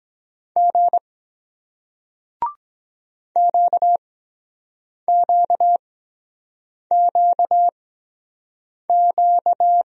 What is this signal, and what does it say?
Morse code, heard as a keyed tone, about 23 wpm, spaced as for 7 wpm, 700 Hz. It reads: G QQQQ